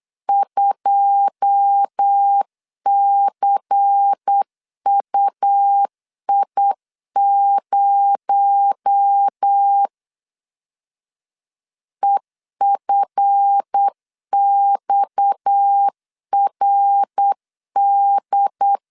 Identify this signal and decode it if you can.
Morse code, heard as a keyed tone, about 8 wpm, 800 Hz. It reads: 2CUI0 EFXRD
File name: LEZIONI / 9 2 8wpm 3437